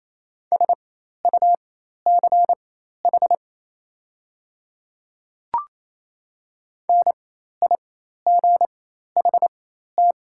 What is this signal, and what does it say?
Morse code, heard as a keyed tone, about 28 wpm, spaced as for 13 wpm, 690 Hz. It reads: SUCH NIGHT
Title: 4x Character Spacing - Top 300 Words - 28wpm